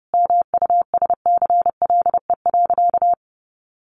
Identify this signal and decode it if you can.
Morse code, heard as a keyed tone, about 30 wpm, 700 Hz. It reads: MUSCLE.
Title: Sentences from Top 1300 Words 30wpm